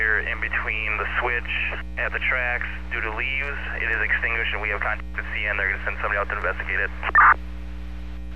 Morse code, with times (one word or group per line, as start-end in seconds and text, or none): none